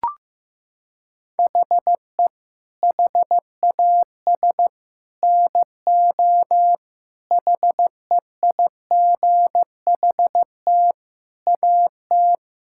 Morse code, heard as a keyed tone, about 15 wpm, 700 Hz. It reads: HE HAS NO HEIGHT AT
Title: Sentences from Top 1600 Words 15wpm